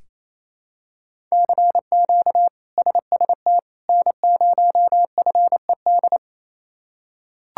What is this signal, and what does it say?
Morse code, heard as a keyed tone, about 28 wpm, 700 Hz. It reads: CQ SST N0FED
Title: K1USN SST QSO Practice - 28wpm